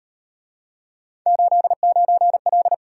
Morse code, heard as a keyed tone, about 38 wpm, 700 Hz. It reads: 89L